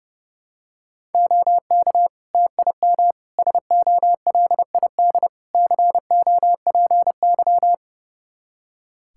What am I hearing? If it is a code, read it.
Morse code, heard as a keyed tone, about 32 wpm, 700 Hz. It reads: OK TIM SOLID COPY
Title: Practice QSO - Exchange 3 - More Info - 30wpm